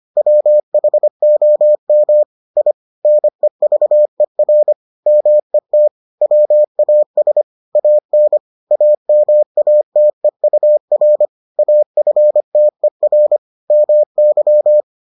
Morse code, heard as a keyed tone, about 25 wpm, 600 Hz. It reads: WHOM I NEVER MET WAS AN AMATEUR AFTER MY